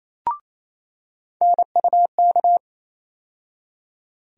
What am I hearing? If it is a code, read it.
Morse code, heard as a keyed tone, about 28 wpm, 700 Hz. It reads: NUK
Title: USA Call Sign Postfixes - 3 Letters 28wpm